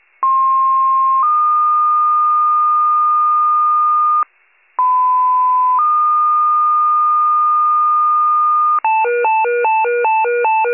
Two-tone page, heard as a one-second tone, then a three-second tone, about 1060 then 1220 Hz, about 1010 then 1220 Hz. It is heard from 0.2 to 8.8 s.
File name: pager / 94 RESCUE / wcalert